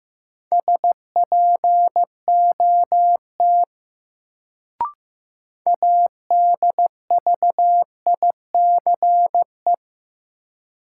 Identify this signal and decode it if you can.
Morse code, heard as a keyed tone, about 15 wpm, 700 Hz. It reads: SPOT ADVICE
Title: Sentences from Top 1100 Words - New Word Drill - 15wpm